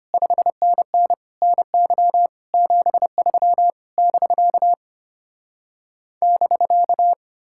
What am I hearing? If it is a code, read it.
Morse code, heard as a keyed tone, about 33 wpm, 700 Hz. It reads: 5NN NY 73 <BK> <BK>